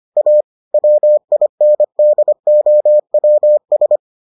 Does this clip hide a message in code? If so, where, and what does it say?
Yes: Morse code from 0.2 to 4.0 s, heard as a keyed tone, about 25 wpm, 600 Hz. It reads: A WINDOWS